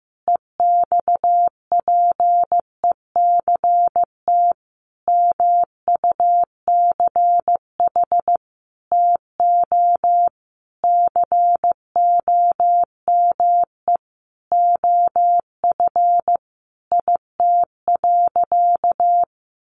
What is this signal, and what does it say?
Morse code, heard as a keyed tone, about 15 wpm, 700 Hz. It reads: EXPECT MUCH TO COME OF IT.
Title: Sentences from Top 1100 Words 15wpm